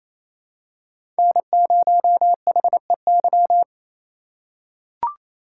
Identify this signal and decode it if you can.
Morse code, heard as a keyed tone, about 28 wpm, 700 Hz. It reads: N0HEY
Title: USA Call Signs - 5 Characters - 4x Character Spacing - 28wpm